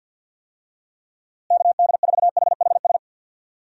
Morse code, heard as a keyed tone, about 53 wpm, 700 Hz. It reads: KD4HHS
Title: USA Call Signs - 6 Characters - Repeated 2x - 50wpm